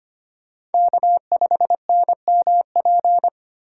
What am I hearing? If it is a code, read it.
Morse code, heard as a keyed tone, about 32 wpm, 700 Hz. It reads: K5NMP